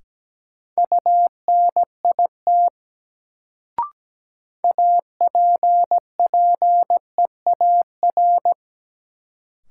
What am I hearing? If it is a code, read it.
Morse code, heard as a keyed tone, about 17 wpm, 700 Hz. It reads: UNIT APPEAR
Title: Sentences from Top 1000 Words - Review Part 2 - New Word Drill - 17wpm